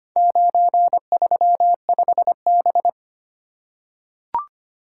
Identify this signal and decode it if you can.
Morse code, heard as a keyed tone, about 25 wpm, 700 Hz. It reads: 935B
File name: practice / 25wpm / Four Letter-Number Combinations - 25wpm